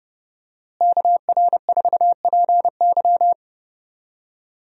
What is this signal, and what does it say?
Morse code, heard as a keyed tone, about 32 wpm, 700 Hz. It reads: KR4PY
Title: USA Call Signs - 5 Characters - Repeated 2x - 30wpm